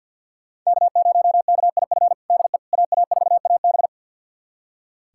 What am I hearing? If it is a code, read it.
Morse code, heard as a keyed tone, about 46 wpm, 700 Hz. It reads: K0KIF DE AA4AB
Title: USA Callsign DE Callsign - 5 Characters - 50wpm